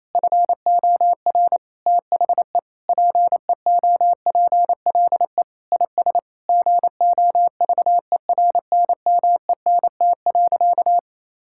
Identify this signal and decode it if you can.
Morse code, heard as a keyed tone, about 28 wpm, 700 Hz. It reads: FOR THE PEOPLE IS GOVERNMENT.
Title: Sentences from Top 1000 Words - Review Part 6 - 28wpm